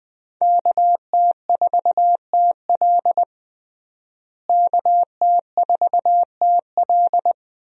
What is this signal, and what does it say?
Morse code, heard as a keyed tone, about 20 wpm, 700 Hz. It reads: KT4TL KT4TL
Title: USA Call Signs - 5 Characters - Repeated 2x - 20wpm